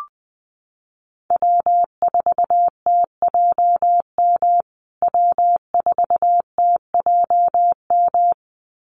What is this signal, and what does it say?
Morse code, heard as a keyed tone, about 20 wpm, 700 Hz. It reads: W4TJM W4TJM
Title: USA Call Signs Repeated 2x 20wpm